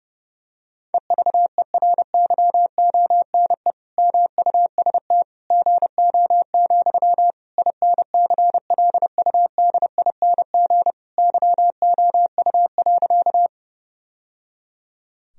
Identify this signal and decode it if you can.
Morse code, heard as a keyed tone, about 30 wpm, 700 Hz. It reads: EVERYONE MUST GO, INCLUDING YOU.
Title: Sentences from Top 1000 Words - Review Part 4 - 30wpm